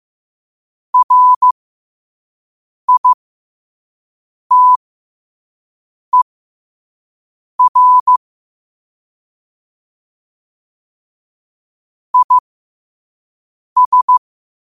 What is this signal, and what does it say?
Morse code, heard as a keyed tone, about 15 wpm, spaced as for 5 wpm, 1000 Hz. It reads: RITER IS
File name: qotdfiles / 5wpm